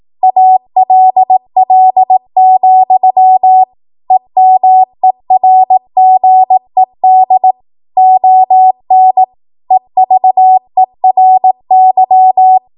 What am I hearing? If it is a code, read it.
Morse code, heard as a keyed tone, about 18 wpm, 750 Hz. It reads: ALL, EMERGED ON EVERY
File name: Archive / 18 WPM / 18WPM